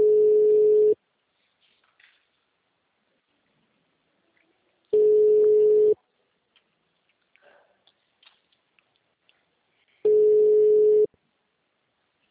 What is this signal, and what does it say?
Ringback tone (Europe, 425 Hz).
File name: out- -2026- - - .33544